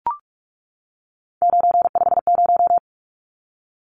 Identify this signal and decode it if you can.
Morse code, heard as a keyed tone, about 44 wpm, 700 Hz. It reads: 950
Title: Numbers - 3-Digit 45wpm